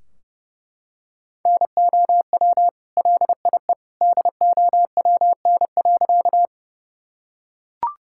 Morse code, heard as a keyed tone, about 30 wpm, 700 Hz. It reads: NOW LIE DOWN.
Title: Sentences from Top 1600 Words 30wpm